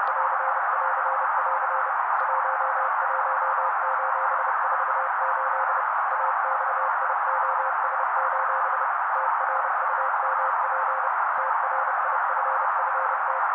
Morse code, heard as a keyed tone, about 30 wpm, 530 Hz. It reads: <AR><AR><AR> 11:48 AXIOS: TRUMP TRIFFT
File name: daily / 29wpm noise